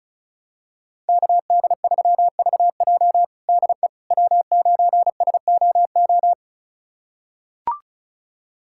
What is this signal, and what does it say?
Morse code, heard as a keyed tone, about 36 wpm, 700 Hz. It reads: KD3VJ DE W9SOO
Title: USA Callsign DE Callsign - 5 Characters - 35wpm